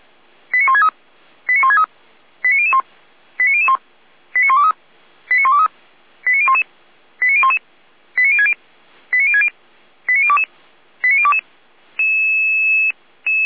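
Five-tone selective call (ZVEI), heard as quick runs of five tones, tones of about 70 ms. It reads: 89163 89163 89001 89001 89123 89123 89011 89011 89060 89060 89022 89022
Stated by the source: ZVEI five-tone selective call received via SDR.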